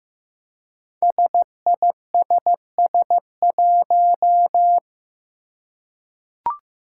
Morse code, heard as a keyed tone, about 15 wpm, 700 Hz. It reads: SISS1
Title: Five Challenging Letter-Number Combinations 15wpm